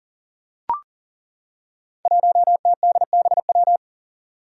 Morse code, heard as a keyed tone, about 39 wpm, 700 Hz. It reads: 1TDBW
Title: Five Challenging Letter-Number Combinations 40wpm